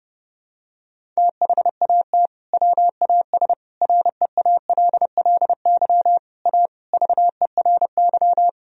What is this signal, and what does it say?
Morse code, heard as a keyed tone, about 32 wpm, 700 Hz. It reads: THAT WAS REALLY A VERY
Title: Sentences from Top 1000 Words - Review Part 1 - 30wpm